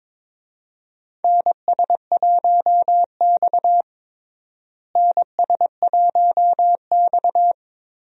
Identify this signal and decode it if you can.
Morse code, heard as a keyed tone, about 21 wpm, 700 Hz. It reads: NS1X NS1X
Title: USA Call Signs - 4 Characters - Repeated 2x - 22wpm